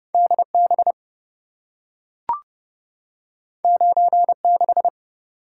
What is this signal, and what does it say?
Morse code, heard as a keyed tone, about 28 wpm, 700 Hz. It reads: DB 96